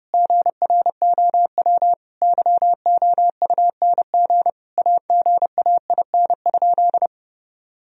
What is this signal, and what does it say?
Morse code, heard as a keyed tone, about 32 wpm, 700 Hz. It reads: GROW YOUNG AGAIN?
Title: Sentences from Top 1000 Words - Review Part 1 - 30wpm